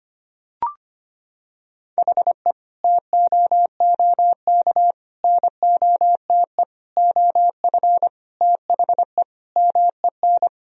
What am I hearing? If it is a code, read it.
Morse code, heard as a keyed tone, about 25 wpm, 700 Hz. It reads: HE TOOK NOTE OF THE MEN
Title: Sentences from Top 1000 Words - Review Part 2 - 25wpm